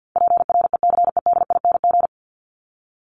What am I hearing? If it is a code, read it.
Morse code, heard as a keyed tone, about 50 wpm, 700 Hz. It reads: PRECEDING
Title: ING Suffix - Top 100 Words with Base Words 50wpm